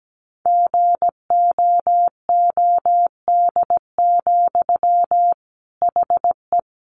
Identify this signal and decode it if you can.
Morse code, heard as a keyed tone, about 17 wpm, 700 Hz. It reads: GOOD, HE
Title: Sentences from Top 600 Words 17wpm